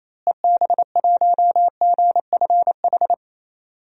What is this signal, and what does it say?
Morse code, heard as a keyed tone, about 28 wpm, 700 Hz. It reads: EB1GFH
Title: DXCC-100 Call Signs - 6 Characters - Repeated 3x - 28wpm